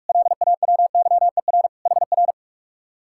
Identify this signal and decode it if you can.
Morse code, heard as a keyed tone, about 58 wpm, 700 Hz. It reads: LAWYER HR